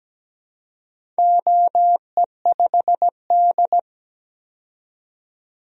Morse code, heard as a keyed tone, about 17 wpm, 700 Hz. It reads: OE5D
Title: DXCC-100 Call Signs - 4 Characters - 17wpm